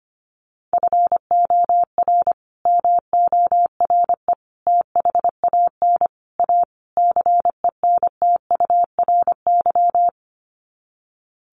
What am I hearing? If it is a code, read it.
Morse code, heard as a keyed tone, about 25 wpm, 700 Hz. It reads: FOR MORE THAN A CENTURY
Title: Sets of 5 Words Encore2 25wpm